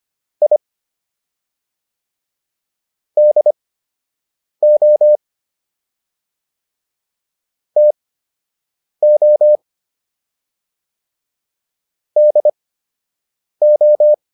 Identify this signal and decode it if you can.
Morse code, heard as a keyed tone, about 25 wpm, spaced as for 7 wpm, 600 Hz. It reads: I DO TO DO